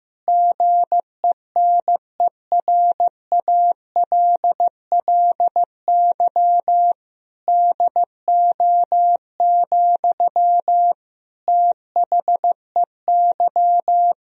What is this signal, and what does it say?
Morse code, heard as a keyed tone, about 16 wpm, 700 Hz. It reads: GENERALLY DO, THEY
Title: Sentences from Top 1000 Words - Review Part 4 - 15wpm